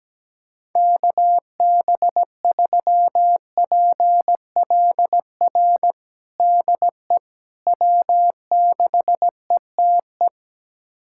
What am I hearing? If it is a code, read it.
Morse code, heard as a keyed tone, about 17 wpm, 700 Hz. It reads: KB3PLR DE W6ETE
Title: USA Callsign DE Callsign 17wpm